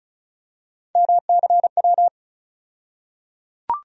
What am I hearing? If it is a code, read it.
Morse code, heard as a keyed tone, about 34 wpm, 700 Hz. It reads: MCW
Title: USA Call Sign Postfixes - 3 Letters 35wpm